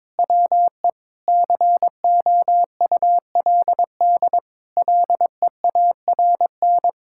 Morse code, heard as a keyed tone, about 22 wpm, 700 Hz. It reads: WE COULD LEARN
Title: Sentences from Top 1600 Words 22wpm